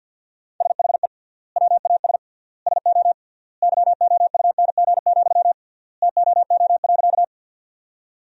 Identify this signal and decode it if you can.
Morse code, heard as a keyed tone, about 50 wpm, 700 Hz. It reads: SHE WAS SO YOUNG, TOO.
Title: 3x Word-Spacing - Sentences from Top 400 Words - 50wpm